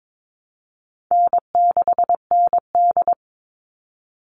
Morse code, heard as a keyed tone, about 22 wpm, 700 Hz. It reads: N6ND
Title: All Cut Numbers - 4-Digit 22wpm